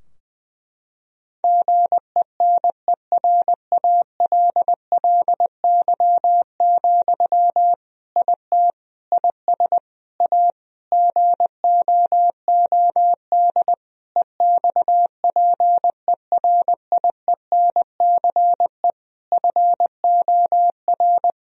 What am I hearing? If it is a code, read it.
Morse code, heard as a keyed tone, about 21 wpm, 700 Hz. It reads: GENERALLY, IT IS A GOOD EXPERIENCE FOR
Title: Sentences from Top 1000 Words - Review Part 4 - 20wpm